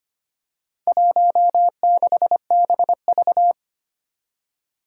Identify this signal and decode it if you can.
Morse code, heard as a keyed tone, about 25 wpm, 700 Hz. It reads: 16BV
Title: Four Challenging Letter-Number Combinations 25wpm